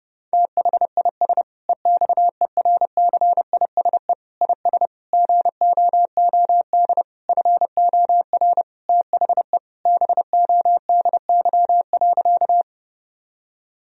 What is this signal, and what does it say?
Morse code, heard as a keyed tone, about 32 wpm, 700 Hz. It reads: THIS EXERCISE IS GOOD FOR THE BODY.
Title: Sentences from Top 1000 Words - Review Part 4 - 30wpm